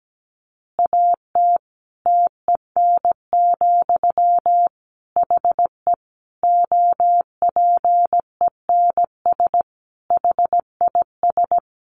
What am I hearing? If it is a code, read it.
Morse code, heard as a keyed tone, about 17 wpm, 700 Hz. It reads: AT TEN, HE OPENS HIS
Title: Easy Phrases 17wpm